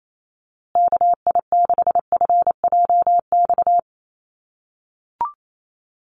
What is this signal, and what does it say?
Morse code, heard as a keyed tone, about 28 wpm, 700 Hz. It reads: KI6FJX
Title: USA Call Signs Repeated 3x 28wpm